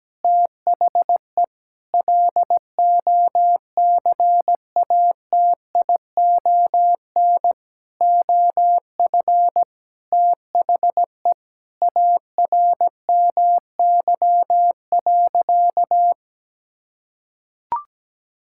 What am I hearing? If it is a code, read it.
Morse code, heard as a keyed tone, about 17 wpm, 700 Hz. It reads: THE LOCATION OF THE ARMY.
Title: Sentences from Top 1000 Words - Review Part 6 - 17wpm